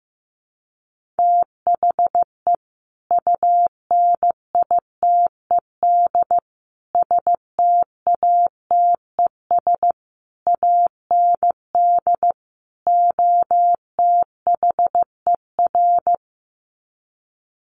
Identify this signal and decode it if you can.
Morse code, heard as a keyed tone, about 15 wpm, 700 Hz. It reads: THE UNITED STATES AND OTHER
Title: Sets of 5 Words Encore 15wpm